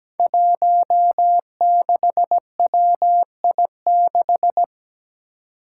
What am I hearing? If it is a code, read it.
Morse code, heard as a keyed tone, about 17 wpm, 700 Hz. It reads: 16WI6
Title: Five Challenging Letter-Number Combinations 17wpm